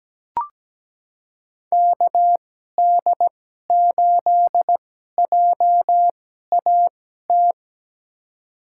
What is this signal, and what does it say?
Morse code, heard as a keyed tone, about 17 wpm, spaced as for 12 wpm, 700 Hz. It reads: KD8JAT
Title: USA Call Signs - 6 Characters - 2x Character Spacing - 17wpm